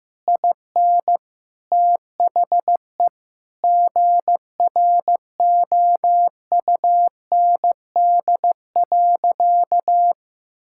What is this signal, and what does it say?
Morse code, heard as a keyed tone, about 16 wpm, 700 Hz. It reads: IN THE GROUND.
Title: Sentences from Top 1600 Words 15wpm